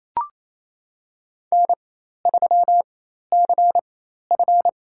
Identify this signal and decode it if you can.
Morse code, heard as a keyed tone, about 28 wpm, spaced as for 13 wpm, 700 Hz. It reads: N3CF
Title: USA Call Signs - 4 Characters - 4x Character Spacing - 28wpm